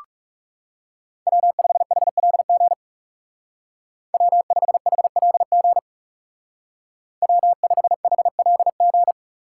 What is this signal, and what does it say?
Morse code, heard as a keyed tone, about 34 wpm, 700 Hz. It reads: W5HLG W5HLG W5HLG
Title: Speed Racing - USA 1x3 Call Signs 30wpm